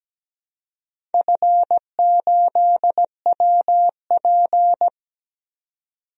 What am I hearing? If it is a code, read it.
Morse code, heard as a keyed tone, about 17 wpm, 700 Hz. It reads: F8WP
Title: DXCC-100 Call Signs - 4 Characters - 17wpm